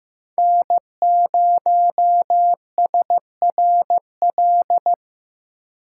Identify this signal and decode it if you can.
Morse code, heard as a keyed tone, about 15 wpm, 700 Hz. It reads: N0SRL
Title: USA Call Signs - 5 Characters - 2x Character Spacing - 15wpm